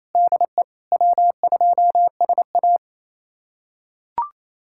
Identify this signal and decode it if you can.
Morse code, heard as a keyed tone, about 28 wpm, 700 Hz. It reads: DE W2SA
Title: USA Callsign DE Callsign - 4 Characters - 28wpm